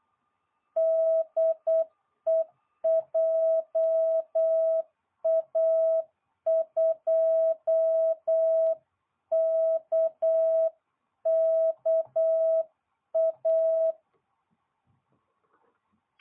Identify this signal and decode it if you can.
Morse code, heard as a keyed tone, about 8 wpm, 660 Hz. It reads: DE JA2KKA